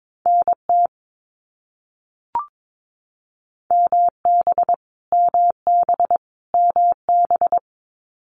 Morse code, heard as a keyed tone, about 22 wpm, 700 Hz. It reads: NT MB MB MB
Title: Canadian Provinces and Territories - Repeated 3x - 22wpm